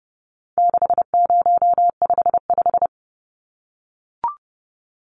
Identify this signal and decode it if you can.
Morse code, heard as a keyed tone, about 30 wpm, 700 Hz. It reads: 6055
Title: Numbers - 4-Digit 30wpm